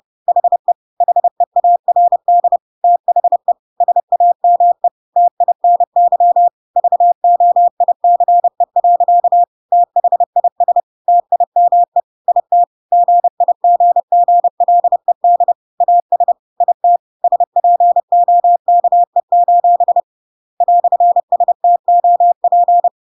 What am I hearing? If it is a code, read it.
Morse code, heard as a keyed tone, about 30 wpm, 700 Hz. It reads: HE HEARD THE SAME TINY VOICE. THIS TIME IT GIGGLED AS IT SPOKE: "STOP
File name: Pinocchio / Pinocchio ch1 30 30 ik1hge